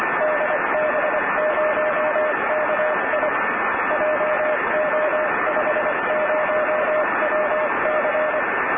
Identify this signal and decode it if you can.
Morse code, heard as a keyed tone, about 24 wpm, 610 Hz, with radio noise. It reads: DB0KI JO50WC
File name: db0ki